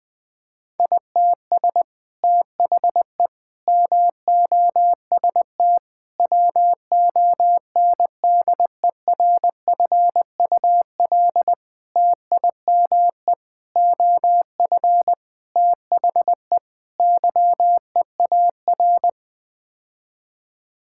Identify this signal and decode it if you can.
Morse code, heard as a keyed tone, about 20 wpm, 700 Hz. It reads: ITS THE MOST WONDERFUL TIME OF THE YEAR
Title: Sets of 8 Words - 20wpm